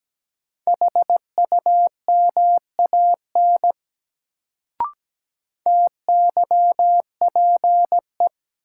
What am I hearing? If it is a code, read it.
Morse code, heard as a keyed tone, about 17 wpm, 700 Hz. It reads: HUMAN TYPE
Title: Sentences from Top 600 Words - New Word Drill - 17wpm